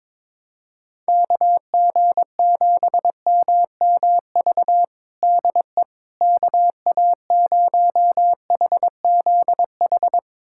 Morse code, heard as a keyed tone, about 22 wpm, 700 Hz. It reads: KG7MMV DE KA0HZH
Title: USA Callsign DE Callsign 22wpm